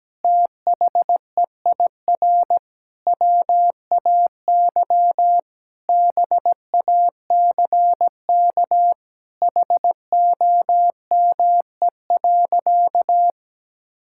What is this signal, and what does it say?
Morse code, heard as a keyed tone, about 17 wpm, 700 Hz. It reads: THEIR WAY BACK HOME.
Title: Sentences from Top 1000 Words - Review Part 6 - 17wpm